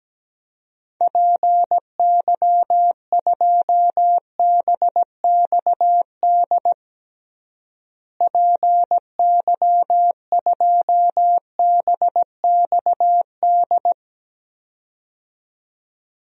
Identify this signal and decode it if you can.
Morse code, heard as a keyed tone, about 17 wpm, 700 Hz. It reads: PY2BXD PY2BXD
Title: DXCC-100 Call Signs - 6 Characters - Repeated 2x - 17wpm